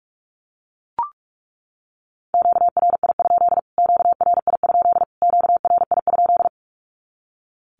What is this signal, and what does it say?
Morse code, heard as a keyed tone, about 44 wpm, 700 Hz. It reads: QRI? QRI? QRI?
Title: Q-Codes Repeated 3x 45wpm